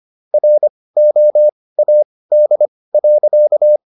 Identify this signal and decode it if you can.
Morse code, heard as a keyed tone, about 25 wpm, spaced as for 18 wpm, 600 Hz. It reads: ROAD.